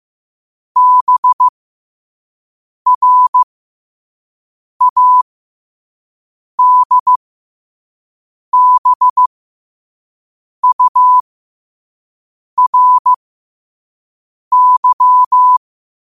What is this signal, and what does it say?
Morse code, heard as a keyed tone, about 15 wpm, spaced as for 5 wpm, 1000 Hz. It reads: BRADBURY
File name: qotdfiles / 5wpm